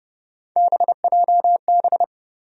Morse code, heard as a keyed tone, about 30 wpm, 700 Hz. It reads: BJB
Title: USA Call Sign Postfixes - 3 Letters 30wpm